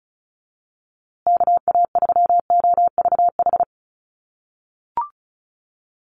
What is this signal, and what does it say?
Morse code, heard as a keyed tone, about 34 wpm, 700 Hz. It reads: KA3OVH
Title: USA Call Signs Repeated 2x 35wpm